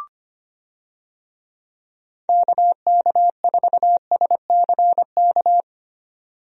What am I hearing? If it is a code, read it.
Morse code, heard as a keyed tone, about 25 wpm, 700 Hz. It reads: KK4SCK
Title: USA Call Signs - 6 Characters - 25wpm